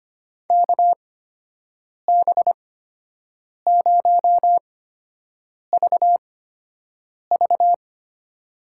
Morse code, heard as a keyed tone, about 25 wpm, spaced as for 7 wpm, 700 Hz. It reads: KB0VV
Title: USA Call Signs - 5 Characters - 8x Character Spacing - 25wpm